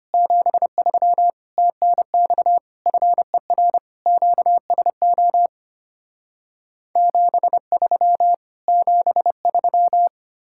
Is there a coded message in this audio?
Yes: Morse code from 0.1 to 10.1 s, heard as a keyed tone, about 33 wpm, 700 Hz. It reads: 73 TNX FER QSO 73 73